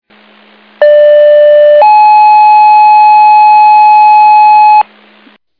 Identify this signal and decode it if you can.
Two-tone page, heard as a one-second tone, then a three-second tone, about 590 then 850 Hz, with noise. It is heard from 0.8 to 4.8 s.